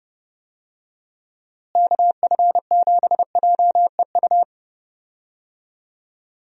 Morse code, heard as a keyed tone, about 30 wpm, 700 Hz. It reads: KF7JEU
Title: USA Call Signs - 6 Characters - Repeated 3x - 30wpm